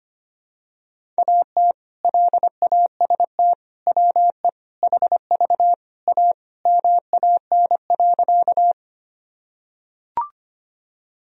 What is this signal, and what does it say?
Morse code, heard as a keyed tone, about 25 wpm, 700 Hz. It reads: AT LAST WE HV A MAN.
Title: Sentences from Top 300 Words and Abbreviations 25wpm